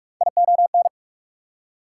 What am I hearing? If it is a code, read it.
Morse code, heard as a keyed tone, about 45 wpm, 690 Hz. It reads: ION
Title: Prefixes and Suffixes 45wpm